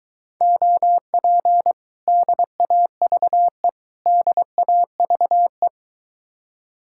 Morse code, heard as a keyed tone, about 22 wpm, 700 Hz. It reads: OP DAVE DAVE